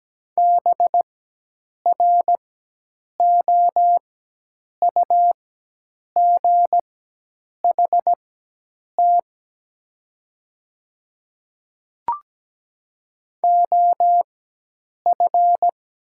Morse code, heard as a keyed tone, about 17 wpm, spaced as for 8 wpm, 700 Hz. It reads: BROUGHT OF
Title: 4x Character Spacing - Top 500 Words - 17wpm